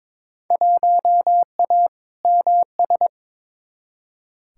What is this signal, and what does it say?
Morse code, heard as a keyed tone, about 22 wpm, 700 Hz. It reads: 1A MS
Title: Field Day Exchanges - 22wpm